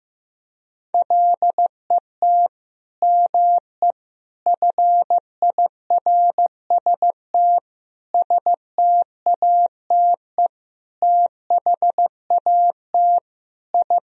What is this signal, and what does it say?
Morse code, heard as a keyed tone, about 16 wpm, 700 Hz. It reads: LET ME FIRST STATE THAT I
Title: Sentences from Top 1000 Words - Review Part 2 - 15wpm